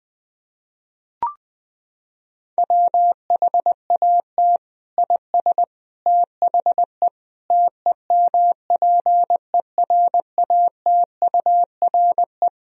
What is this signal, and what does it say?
Morse code, heard as a keyed tone, about 20 wpm, 700 Hz. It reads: WHAT IS THE TEMPERATURE
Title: Sentences from Top 1000 Words - Review Part 4 - 20wpm